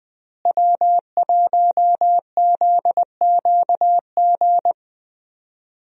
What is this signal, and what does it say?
Morse code, heard as a keyed tone, about 20 wpm, 700 Hz. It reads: W1ZQG
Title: USA Call Signs - 5 Characters - Repeated 3x - 20wpm